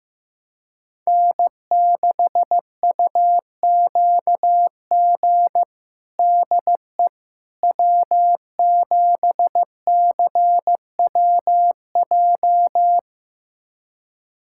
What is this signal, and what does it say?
Morse code, heard as a keyed tone, about 15 wpm, 700 Hz. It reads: N6UQG DE W7CWJ
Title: USA Callsign DE Callsign - 5 Characters - 15wpm